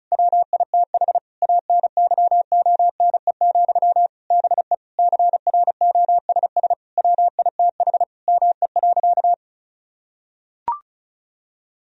Morse code, heard as a keyed tone, about 35 wpm, 700 Hz. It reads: WITH ANYONE, BE CROSS WITH ME.
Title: Sentences from Top 1000 Words - Review Part 1 - 35wpm